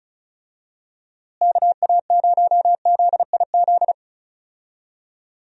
Morse code, heard as a keyed tone, about 36 wpm, 700 Hz. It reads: KA0ZIZ
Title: USA Call Signs - 6 Characters - Repeated 3x - 35wpm